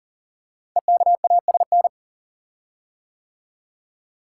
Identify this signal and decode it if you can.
Morse code, heard as a keyed tone, about 38 wpm, 690 Hz. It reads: EKASN
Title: Code Groups - 5 Letters - 40wpm